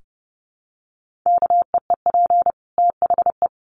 Morse code, heard as a keyed tone, about 30 wpm, 700 Hz. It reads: KEEP THE
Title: Sentences from Top 1500 Words 30wpm